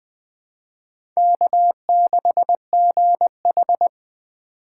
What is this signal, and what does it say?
Morse code, heard as a keyed tone, about 20 wpm, 700 Hz. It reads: K6GH
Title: USA Call Signs - 4 Characters - 4x Character Spacing - 20wpm